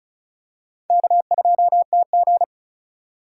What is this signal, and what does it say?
Morse code, heard as a keyed tone, about 34 wpm, 700 Hz. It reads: K2TG
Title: USA Call Signs - 4x Character Spacing - 35wpm